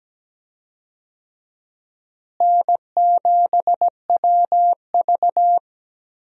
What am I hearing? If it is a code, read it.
Morse code, heard as a keyed tone, about 17 wpm, 700 Hz. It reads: N7WV
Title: USA Call Signs - 4 Characters - Repeated 2x - 17wpm